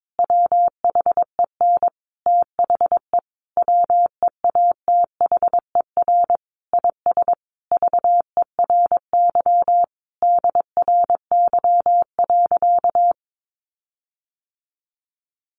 Morse code, heard as a keyed tone, about 22 wpm, 700 Hz. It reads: WHEN THE WEATHER IS VERY DRY.
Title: Sentences from Top 1400 Words 22wpm